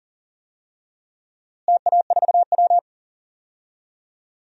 Morse code, heard as a keyed tone, about 40 wpm, 700 Hz. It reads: TA4W
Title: DXCC-100 Call Signs - 4 Characters - Repeated 3x - 40wpm